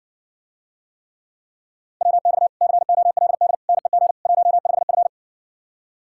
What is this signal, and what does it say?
Morse code, heard as a keyed tone, about 60 wpm, 700 Hz. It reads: WX COLD NEG 15F